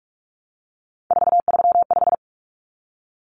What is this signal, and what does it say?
Morse code, heard as a keyed tone, about 44 wpm, 700 Hz. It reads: 435
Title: Common Cut Numbers - 3-Digit 45wpm